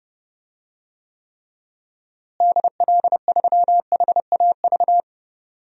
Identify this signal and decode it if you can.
Morse code, heard as a keyed tone, about 32 wpm, 700 Hz. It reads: DL3HAV
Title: DXCC-100 Call Signs - 6 Characters - Repeated 2x - 30wpm